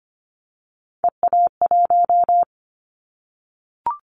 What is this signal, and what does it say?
Morse code, heard as a keyed tone, about 25 wpm, 700 Hz. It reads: EA1
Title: Callsign Prefixes for Top 25 DXCC - 25wpm